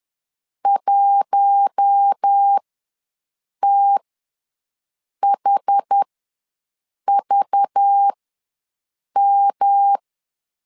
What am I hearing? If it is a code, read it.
Morse code, heard as a keyed tone, about 10 wpm, spaced as for 6 wpm, 800 Hz. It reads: 1THVM
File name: LEZIONI / 5 LBVC / 5 LBV ws 10wpm 3645